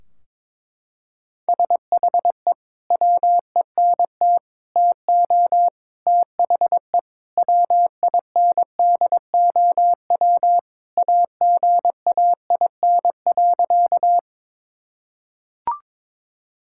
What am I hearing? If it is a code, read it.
Morse code, heard as a keyed tone, about 22 wpm, 700 Hz. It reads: SHE WENT TO THE WINDOW AGAIN.
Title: Sentences from Top 1600 Words 22wpm